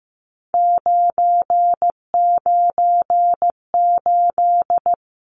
Morse code, heard as a keyed tone, about 15 wpm, 700 Hz. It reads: 998
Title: Numbers - 3-Digit 15wpm